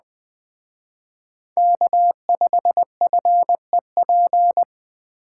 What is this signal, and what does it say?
Morse code, heard as a keyed tone, about 20 wpm, 700 Hz. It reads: K5FEP
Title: Speed Racing - USA 1x3 Call Signs 17wpm